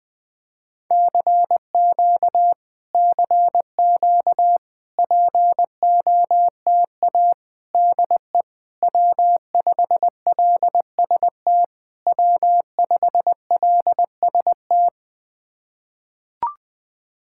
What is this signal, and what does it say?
Morse code, heard as a keyed tone, about 21 wpm, 700 Hz. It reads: CQ CQ POTA DE W5LST W5LST
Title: POTA QSO Practice - 20wpm